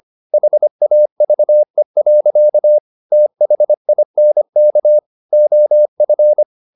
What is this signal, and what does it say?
Morse code, heard as a keyed tone, about 25 wpm, 600 Hz. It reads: HAVE. THINK OF